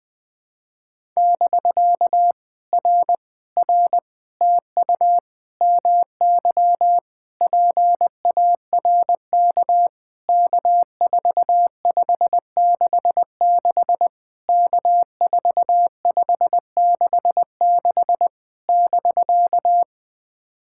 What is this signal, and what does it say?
Morse code, heard as a keyed tone, about 21 wpm, 700 Hz. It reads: <BK> R R TU MY PARK K4566 K4566 <BK>
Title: POTA QSO Practice - 20wpm